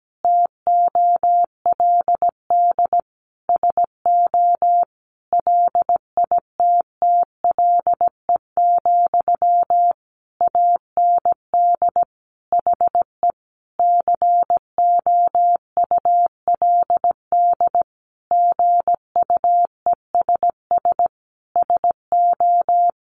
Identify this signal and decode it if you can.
Morse code, heard as a keyed tone, about 17 wpm, 700 Hz. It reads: TOLD SO LITTLE, AND HE COULD GUESS SO
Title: Sentences from Top 1500 Words 17wpm